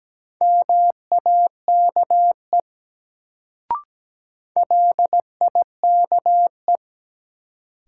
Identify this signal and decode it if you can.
Morse code, heard as a keyed tone, about 17 wpm, 700 Hz. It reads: MAKE LIKE
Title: Sets of 2 Words - New Word Drill - 17wpm